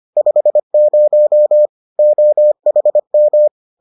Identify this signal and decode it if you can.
Morse code, heard as a keyed tone, about 25 wpm, 600 Hz. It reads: 50 OHM